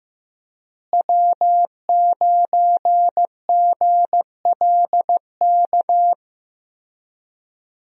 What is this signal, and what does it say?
Morse code, heard as a keyed tone, about 15 wpm, 700 Hz. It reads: W9GLK
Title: USA Call Signs - 5 Characters - Repeated 2x - 15wpm